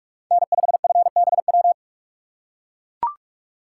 Morse code, heard as a keyed tone, about 43 wpm, 700 Hz. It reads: N5FBW